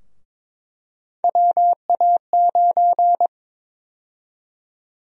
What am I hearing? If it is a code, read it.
Morse code, heard as a keyed tone, about 22 wpm, 700 Hz. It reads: WA9
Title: USA Call Sign Prefixes - 22wpm